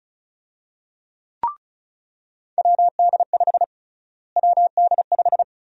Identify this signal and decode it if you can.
Morse code, heard as a keyed tone, about 36 wpm, 700 Hz. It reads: WD5 WD5
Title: USA Call Sign Prefixes - Repeated 2x - 35wpm